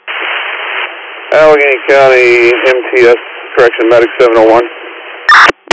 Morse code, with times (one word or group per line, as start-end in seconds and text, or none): none